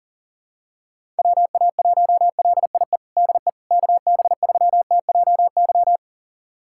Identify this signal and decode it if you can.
Morse code, heard as a keyed tone, about 37 wpm, 700 Hz. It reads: WA1LIE DE KB3TJY
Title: USA Callsign DE Callsign - 6 Characters - 40wpm